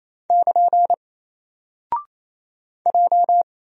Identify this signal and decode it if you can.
Morse code, heard as a keyed tone, about 27 wpm, 700 Hz. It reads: <KN> J
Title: Single Character (Letter:Number:Punctuation:Prosign) 28wpm